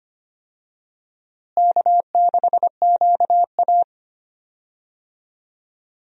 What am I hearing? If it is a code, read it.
Morse code, heard as a keyed tone, about 25 wpm, 700 Hz. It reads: K6QA
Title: USA Call Signs - 4 Characters - Repeated 2x - 25wpm